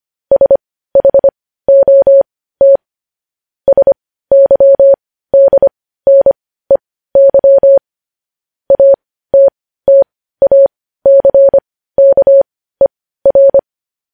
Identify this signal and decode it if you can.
Morse code, heard as a keyed tone, about 25 wpm, spaced as for 15 wpm, 550 Hz. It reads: SHOT SYDNEY ATTACKER
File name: podcast / News headlines-1x-25x15